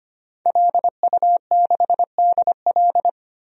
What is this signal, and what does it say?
Morse code, heard as a keyed tone, about 25 wpm, 700 Hz. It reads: LU6DL
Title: DXCC-100 Call Signs - 5 Characters - 25wpm